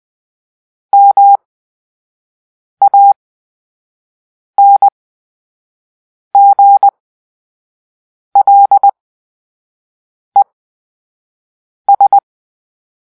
Morse code, heard as a keyed tone, about 20 wpm, spaced as for 5 wpm, 800 Hz. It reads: MANGLES
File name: Grp-3 Drill-4